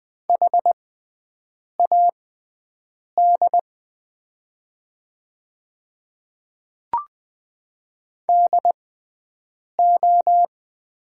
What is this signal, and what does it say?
Morse code, heard as a keyed tone, about 20 wpm, spaced as for 7 wpm, 700 Hz. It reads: HAD DO
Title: 6x Character Spacing - Top 100 Words - 20wpm